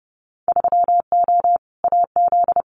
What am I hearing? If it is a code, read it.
Morse code, heard as a keyed tone, about 30 wpm, 700 Hz. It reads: 3O AZ
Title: Winter Field Day Exchanges 30wpm